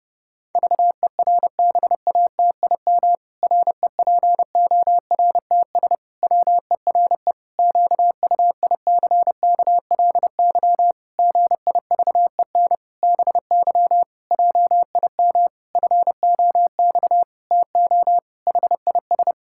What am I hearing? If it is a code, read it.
Morse code, heard as a keyed tone, about 30 wpm, 700 Hz. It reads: VERBATIM REPORTS WERE QUICKLY GIVEN BY JIM FOX TO HIS
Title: Pangram Challenge v2 - 30wpm